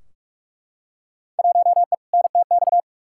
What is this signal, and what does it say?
Morse code, heard as a keyed tone, about 42 wpm, 700 Hz. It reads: 1E NTX
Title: Field Day Exchanges - 45wpm